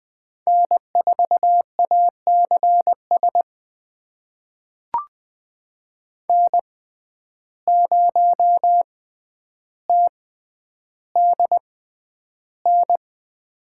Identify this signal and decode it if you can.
Morse code, heard as a keyed tone, about 20 wpm, spaced as for 10 wpm, 700 Hz. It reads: N4ACS N0TDN
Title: USA Call Signs - 6x Character Spacing - 20wpm